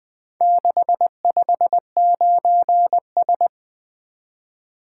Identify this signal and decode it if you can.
Morse code, heard as a keyed tone, about 21 wpm, 700 Hz. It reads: 659S
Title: Four Letter-Number Combinations - 20wpm